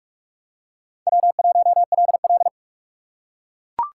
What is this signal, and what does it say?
Morse code, heard as a keyed tone, about 42 wpm, 700 Hz. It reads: W1LL
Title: USA Call Signs - 4 Characters - 4x Character Spacing - 45wpm